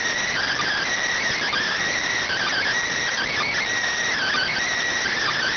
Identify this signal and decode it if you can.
Emergency Alert System (SAME) tones (header bursts), heard as harsh buzzing bursts. It is heard from 0.2 to 4.6 s.